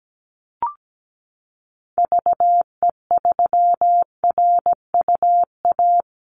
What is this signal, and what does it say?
Morse code, heard as a keyed tone, about 17 wpm, 700 Hz. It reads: VE3RUA
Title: Call Signs Repeated 2x - 17wpm